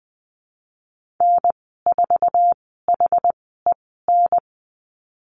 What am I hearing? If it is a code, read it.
Morse code, heard as a keyed tone, about 20 wpm, spaced as for 14 wpm, 700 Hz. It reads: N4HEN
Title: 088 Lesson 15g - Number 5 - Call Signs 20wpm